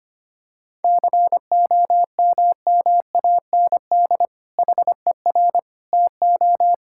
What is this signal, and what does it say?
Morse code, heard as a keyed tone, about 25 wpm, 700 Hz. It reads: COMMAND HER TO
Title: Sentences from Top 1600 Words 25wpm